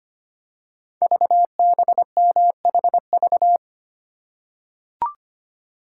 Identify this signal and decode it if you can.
Morse code, heard as a keyed tone, about 25 wpm, 700 Hz. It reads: VBMHV
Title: Code Groups - 5 Letters - 25wpm